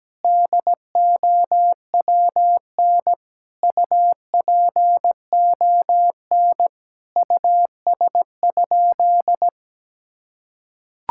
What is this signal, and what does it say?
Morse code, heard as a keyed tone, about 17 wpm, 700 Hz. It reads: DOWN UPON US?
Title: Sentences from Top 1600 Words 17wpm